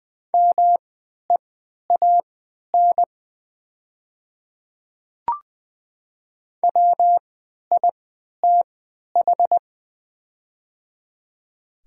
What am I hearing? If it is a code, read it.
Morse code, heard as a keyed tone, about 20 wpm, spaced as for 11 wpm, 690 Hz. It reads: MEAN WITH
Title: 3x Character Spacing - Top 300 Words - 20wpm